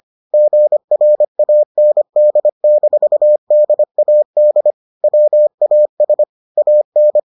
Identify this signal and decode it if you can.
Morse code, heard as a keyed tone, about 25 wpm, 600 Hz. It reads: GRAND-DAD WAS AN